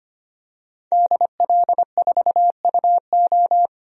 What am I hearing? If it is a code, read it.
Morse code, heard as a keyed tone, about 25 wpm, 700 Hz. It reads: DL4UO
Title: DXCC-100 Call Signs - 5 Characters - 25wpm